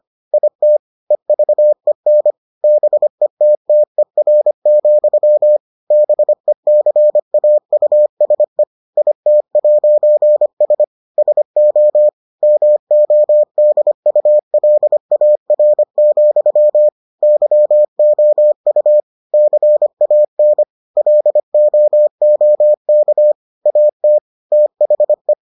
Morse code, heard as a keyed tone, about 25 wpm, 600 Hz. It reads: IT EVEN BETTER, BECAUSE IT'S SO MODULAR, YOU CAN LOOK AT THE